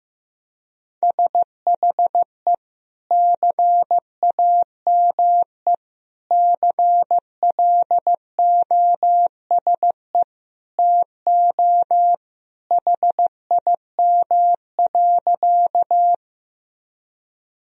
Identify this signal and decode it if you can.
Morse code, heard as a keyed tone, about 15 wpm, 700 Hz. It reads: SHE CAME CLOSE TO HIM.
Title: Sentences from Top 1000 Words - Review Part 1 - 15wpm